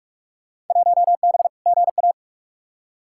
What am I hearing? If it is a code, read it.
Morse code, heard as a keyed tone, about 46 wpm, 700 Hz. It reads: 1B GA
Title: Field Day Exchanges Encore - 45wpm